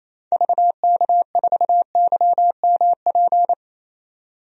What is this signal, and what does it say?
Morse code, heard as a keyed tone, about 27 wpm, 700 Hz. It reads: VK4YMP
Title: DXCC-100 Call Signs - 6 Characters - 28wpm